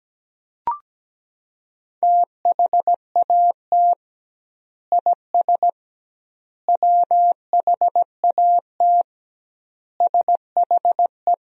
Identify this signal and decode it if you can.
Morse code, heard as a keyed tone, about 17 wpm, 700 Hz. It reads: THAT IS WHAT SHE
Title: 2x Word-Spacing - Sentences from Top 100 Words - 17wpm